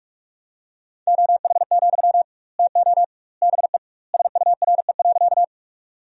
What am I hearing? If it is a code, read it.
Morse code, heard as a keyed tone, about 46 wpm, 700 Hz. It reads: OH, TO BE SURE.
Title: 2x Word-Spacing - Sentences from Top 500 Words - 45wpm